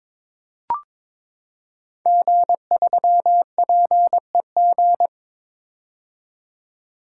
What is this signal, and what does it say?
Morse code, heard as a keyed tone, about 22 wpm, 700 Hz. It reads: G3PEG
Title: Code Groups - 5 Letter:Numbers - 22wpm